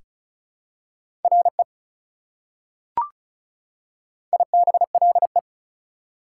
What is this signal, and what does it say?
Morse code, heard as a keyed tone, about 36 wpm, 690 Hz. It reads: RE IBLE
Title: Prefixes and Suffixes 35wpm